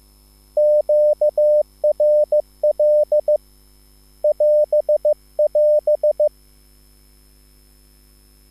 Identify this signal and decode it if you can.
Morse code, heard as a keyed tone, about 15 wpm, 600 Hz. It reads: QRL <AS><AS>
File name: QSL / CW lesson